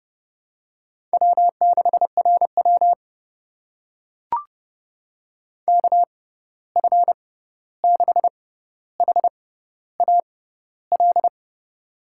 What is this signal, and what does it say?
Morse code, heard as a keyed tone, about 31 wpm, spaced as for 14 wpm, 700 Hz. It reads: W6RW KF6HAL
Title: USA Call Signs - 6x Character Spacing - 30wpm